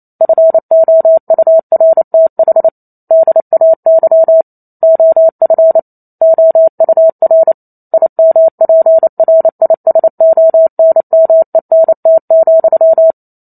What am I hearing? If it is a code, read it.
Morse code, heard as a keyed tone, about 28 wpm, 650 Hz. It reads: FOURTH DAY OF OUR IMPRISONMENT,